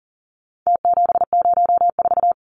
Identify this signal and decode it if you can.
Morse code, heard as a keyed tone, about 40 wpm, 700 Hz. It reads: T704
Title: Numbers - Contest Style - 4-Digit 40wpm